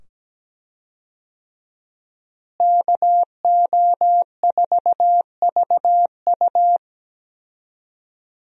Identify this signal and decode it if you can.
Morse code, heard as a keyed tone, about 17 wpm, 700 Hz. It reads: KO4VU
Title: USA Call Signs - 5 Characters - Repeated 2x - 17wpm